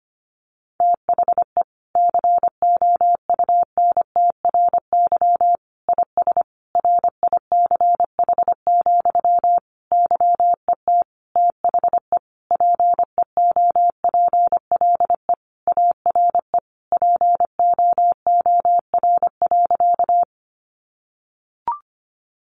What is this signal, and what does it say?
Morse code, heard as a keyed tone, about 25 wpm, 700 Hz. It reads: THE COUNTRY IS RICH, YET THE PEOPLE ARE POOR.
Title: Sentences from Top 1100 Words 25wpm